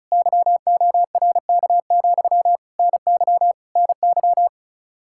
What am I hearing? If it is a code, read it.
Morse code, heard as a keyed tone, about 43 wpm, 700 Hz. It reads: YORK, NY NY
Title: Speed-Racing Common QSO Phrases - 35wpm